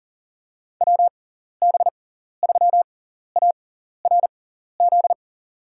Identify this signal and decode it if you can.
Morse code, heard as a keyed tone, about 39 wpm, spaced as for 14 wpm, 700 Hz. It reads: WB3ARZ
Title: USA Call Signs - 6 Characters - 6x Character Spacing - 40wpm